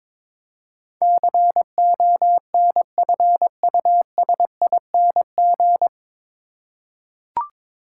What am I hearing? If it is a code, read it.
Morse code, heard as a keyed tone, about 22 wpm, 700 Hz. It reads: CONFUSING
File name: 9 Letter Words - First 3 Letters Given - 22wpm